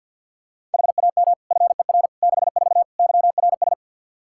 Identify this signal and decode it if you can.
Morse code, heard as a keyed tone, about 53 wpm, 700 Hz. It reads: HAM FER 64 YRS
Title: Practice QSO - Exchange 3 - More Info - 50wpm